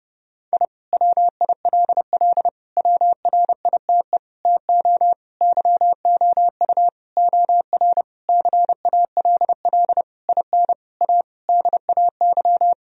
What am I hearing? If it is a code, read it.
Morse code, heard as a keyed tone, about 30 wpm, 700 Hz. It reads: I WILL WRITE TO YOU OR CALL IN A DAY
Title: Sentences from Top 1000 Words - Review Part 1 - 30wpm